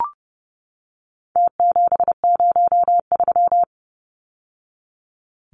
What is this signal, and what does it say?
Morse code, heard as a keyed tone, about 30 wpm, 700 Hz. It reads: T703
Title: Numbers - Contest Style - 4-Digit 30wpm